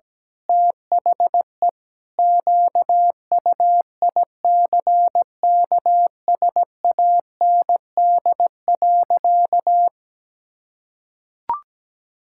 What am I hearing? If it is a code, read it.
Morse code, heard as a keyed tone, about 17 wpm, 700 Hz. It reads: THE QUICKSAND.
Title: Pangram Challenge v2 - 17wpm